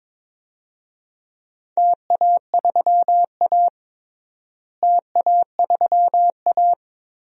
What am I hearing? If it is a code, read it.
Morse code, heard as a keyed tone, about 22 wpm, 700 Hz. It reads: TA3A TA3A
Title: DXCC-100 Call Signs - 4 Characters - Repeated 2x - 22wpm